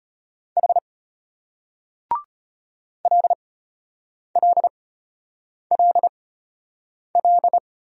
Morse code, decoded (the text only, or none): H LLLL